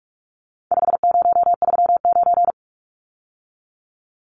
Morse code, heard as a keyed tone, about 44 wpm, 700 Hz. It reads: 5039
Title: Numbers - 4-Digit 45wpm